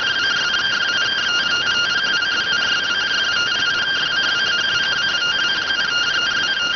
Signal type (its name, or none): two-tone teleprinter signal